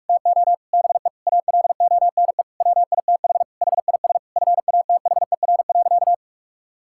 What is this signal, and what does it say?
Morse code, heard as a keyed tone, about 45 wpm, 690 Hz. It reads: TO BE ALONE WITH HIS FATHER.
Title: Sentences from Top 1000 Words - Review Part 5 - 45wpm